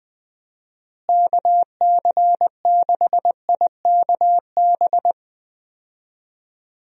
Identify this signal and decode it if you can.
Morse code, heard as a keyed tone, about 20 wpm, 700 Hz. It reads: KC6IKB
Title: USA Call Signs - 6 Characters - Repeated 3x - 20wpm